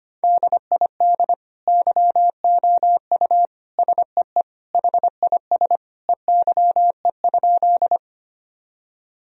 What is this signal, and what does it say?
Morse code, heard as a keyed tone, about 25 wpm, 700 Hz. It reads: DID YOU SEE HIS EYE?
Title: Sentences from Top 1000 Words - Review Part 1 - 25wpm